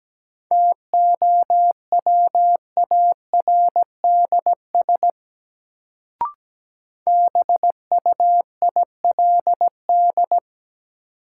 Morse code, heard as a keyed tone, about 17 wpm, 700 Hz. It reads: TOWARDS BUILD
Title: Sentences from Top 1100 Words - New Word Drill Alt - 17wpm